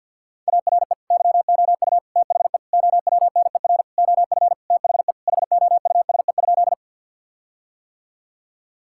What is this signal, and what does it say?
Morse code, heard as a keyed tone, about 50 wpm, 700 Hz. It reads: ARE YOU THE OWNER OF THE HOUSE?
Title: Sentences from Top 1600 Words 50wpm